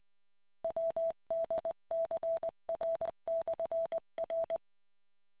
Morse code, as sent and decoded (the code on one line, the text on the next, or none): .-- -.. -.-. .-. -..-. .-.
WDCR/R